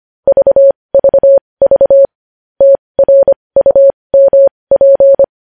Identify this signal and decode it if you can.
Morse code, heard as a keyed tone, about 25 wpm, spaced as for 20 wpm, 550 Hz. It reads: VVV TRUMP
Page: News headlines in morse code at 20 wpm